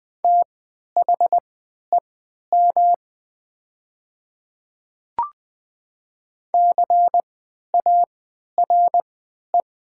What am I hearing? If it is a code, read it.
Morse code, heard as a keyed tone, about 20 wpm, spaced as for 11 wpm, 700 Hz. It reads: THEM CARE
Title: 3x Character Spacing - Top 300 Words - 20wpm